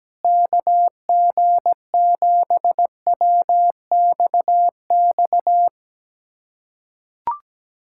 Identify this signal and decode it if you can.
Morse code, heard as a keyed tone, about 17 wpm, 700 Hz. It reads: KG7WXX
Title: USA Callsign DE Callsign 17wpm